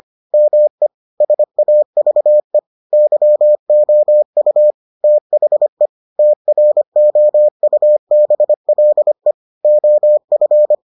Morse code, heard as a keyed tone, about 25 wpm, 600 Hz. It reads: ME SAVE YOU THE TROUBLE OF